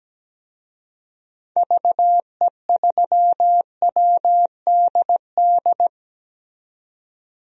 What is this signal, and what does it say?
Morse code, heard as a keyed tone, about 17 wpm, 700 Hz. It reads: VE3WDD
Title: DXCC-100 Call Signs - 6 Characters - 17wpm